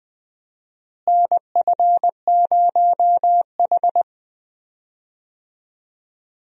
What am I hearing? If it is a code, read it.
Morse code, heard as a keyed tone, about 20 wpm, 700 Hz. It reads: NF0H
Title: USA Call Signs - 4 Characters - Repeated 2x - 20wpm